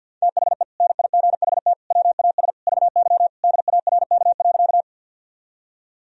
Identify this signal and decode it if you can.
Morse code, heard as a keyed tone, about 51 wpm, 700 Hz. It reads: THE NIGHT WAS VY DARK.
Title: Sentences from Top 500 Words and Abbreviations 50wpm